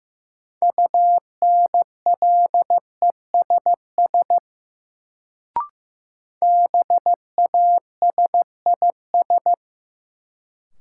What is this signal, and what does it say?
Morse code, heard as a keyed tone, about 15 wpm, 700 Hz. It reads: UNLESS BASIS
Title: Sentences from Top 1000 Words - Review Part 5 - New Word Drill - 15wpm